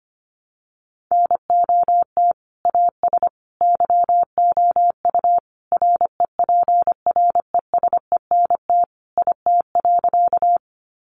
Morse code, heard as a keyed tone, about 25 wpm, 700 Hz. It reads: NOT AS YOU REPRESENT IT.
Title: Sentences from Top 1200 Words 25wpm